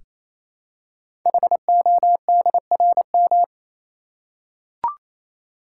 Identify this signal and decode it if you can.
Morse code, heard as a keyed tone, about 29 wpm, 700 Hz. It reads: HODRM
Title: Code Groups - 5 Letters - 28wpm